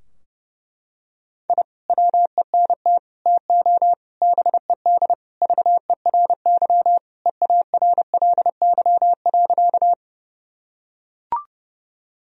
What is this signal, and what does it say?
Morse code, heard as a keyed tone, about 32 wpm, 700 Hz. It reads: I WENT TO BED VERY EARLY.
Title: Sentences from Top 1000 Words - Review Part 2 - 30wpm